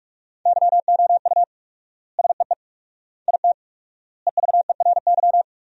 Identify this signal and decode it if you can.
Morse code, heard as a keyed tone, about 46 wpm, 700 Hz. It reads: YOU SEE IT EVERY
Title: 4x Word-Spacing - Sentences from Top 200 Words - 45wpm